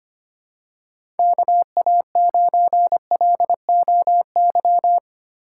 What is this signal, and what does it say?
Morse code, heard as a keyed tone, about 25 wpm, 700 Hz. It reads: KA9LOY
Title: USA Call Signs - 2x Character Spacing - 25wpm